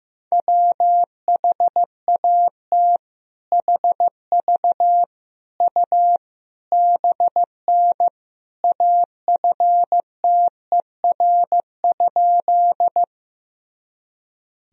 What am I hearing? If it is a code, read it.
Morse code, heard as a keyed tone, about 16 wpm, 700 Hz. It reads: WHAT HV U BN AFTER?
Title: Sentences from Top 200 Words and Abbreviations 15wpm